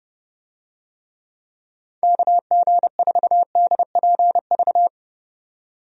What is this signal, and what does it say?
Morse code, heard as a keyed tone, about 31 wpm, 700 Hz. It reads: KG4DPV
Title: USA Call Signs - 6 Characters - 30wpm